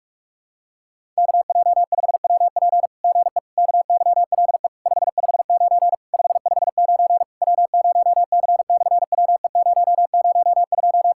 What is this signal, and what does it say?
Morse code, heard as a keyed tone, about 46 wpm, 700 Hz. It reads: KJ5WP GE KYLE 559 559 W0C/WE002
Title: SOTA QSO Practice 45wpm